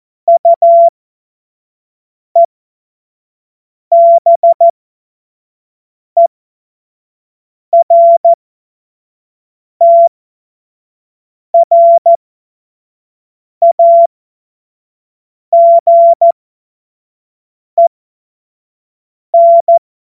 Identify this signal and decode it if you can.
Morse code, heard as a keyed tone, about 14 wpm, spaced as for 5 wpm, 680 Hz. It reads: UEBERTRAGEN